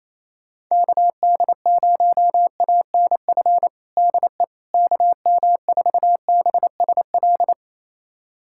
Morse code, heard as a keyed tone, about 28 wpm, 700 Hz. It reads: KD0ANF DE KM4BSL
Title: USA Callsign DE Callsign - 6 Characters - 28wpm